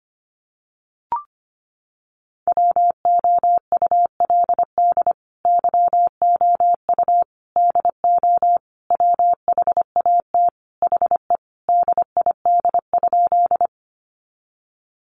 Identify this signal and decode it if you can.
Morse code, heard as a keyed tone, about 25 wpm, 700 Hz. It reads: WOULD YOU DO WHAT HE DID?
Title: Sentences from Top 100 Words 25wpm